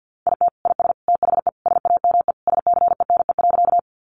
Morse code, heard as a keyed tone, about 50 wpm, 700 Hz. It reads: IT IS THE SAME SCENE.
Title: Sentences from Top 1100 Words 50wpm